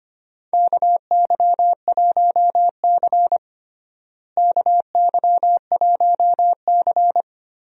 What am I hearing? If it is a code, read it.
Morse code, heard as a keyed tone, about 25 wpm, 700 Hz. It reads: KY1C KY1C
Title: USA Call Signs - 4 Characters - Repeated 2x - 25wpm